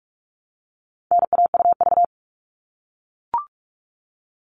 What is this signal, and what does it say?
Morse code, heard as a keyed tone, about 44 wpm, 700 Hz. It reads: NAUV
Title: All Cut Numbers - 4-Digit 45wpm